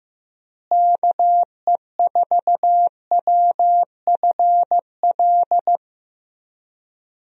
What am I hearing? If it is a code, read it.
Morse code, heard as a keyed tone, about 15 wpm, 700 Hz. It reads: KE4WFL
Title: USA Call Signs - 6 Characters - Repeated 3x - 15wpm